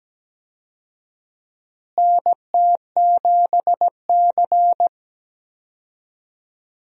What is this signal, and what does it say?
Morse code, heard as a keyed tone, about 17 wpm, 700 Hz. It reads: NT7C
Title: USA Call Signs - 4 Characters - Repeated 3x - 17wpm